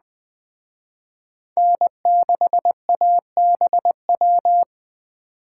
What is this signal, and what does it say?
Morse code, heard as a keyed tone, about 20 wpm, 700 Hz. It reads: N6ABW
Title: Speed Racing - USA 1x3 Call Signs 17wpm